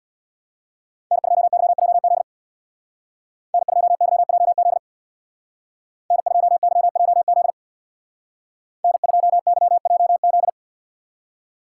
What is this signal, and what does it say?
Morse code, heard as a keyed tone, about 60 wpm, 700 Hz. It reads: N2YJZ N2YJZ N2YJZ N2YJZ
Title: Speed Racing - USA 1x3 Call Signs 50wpm